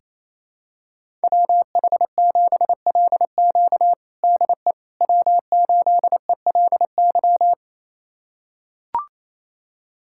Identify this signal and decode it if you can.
Morse code, heard as a keyed tone, about 28 wpm, 700 Hz. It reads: WH7LQ DE W8ELY
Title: USA Callsign DE Callsign - 5 Characters - 28wpm